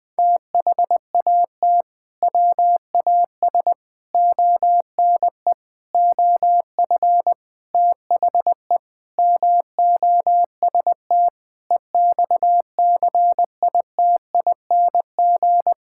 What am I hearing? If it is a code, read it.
Morse code, heard as a keyed tone, about 20 wpm, 700 Hz. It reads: THAT WAS ONE OF THE MOST EXCITING
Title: Sentences with 5-Word Phrases - Spoken Beginning - 20wpm